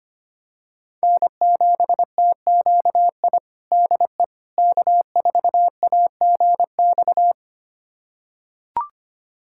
Morse code, heard as a keyed tone, about 25 wpm, 700 Hz. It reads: N7TQI DE K4AGX
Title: USA Callsign DE Callsign 25wpm